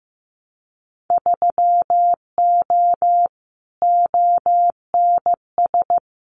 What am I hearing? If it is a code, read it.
Morse code, heard as a keyed tone, about 15 wpm, 700 Hz. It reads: 3O ONS
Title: Winter Field Day Exchanges 15wpm